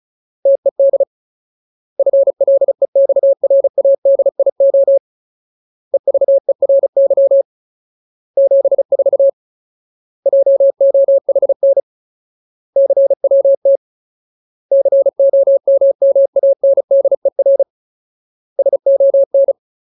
Morse code, heard as a keyed tone, about 34 wpm, 550 Hz. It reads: TED FLEXRADIO EVERY 74 JOHN CWT COMMANDER SON
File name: Ultimate Word List / 35WPM Ultimate